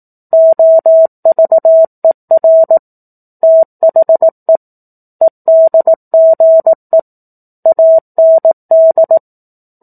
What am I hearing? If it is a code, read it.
Morse code, heard as a keyed tone, about 18 wpm, 650 Hz. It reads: OVER THE EDGE AND
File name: 04-Chapter-04 18 WPM